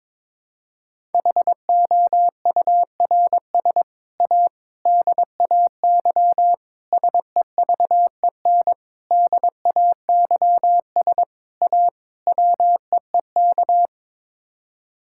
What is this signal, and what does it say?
Morse code, heard as a keyed tone, about 22 wpm, 700 Hz. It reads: HOURS A DAY SEVEN DAYS A WEEK
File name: practice / 22wpm / Sets of 7 Words - 22wpm